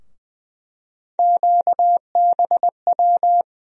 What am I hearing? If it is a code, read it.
Morse code, heard as a keyed tone, about 20 wpm, 700 Hz. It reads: QBW
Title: USA Call Sign Postfixes - 3 Letters 20wpm